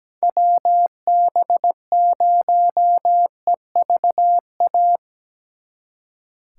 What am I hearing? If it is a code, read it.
Morse code, heard as a keyed tone, about 19 wpm, 700 Hz. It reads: WB0EVA
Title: Speed Racing - USA 2x3 Call Signs 17wpm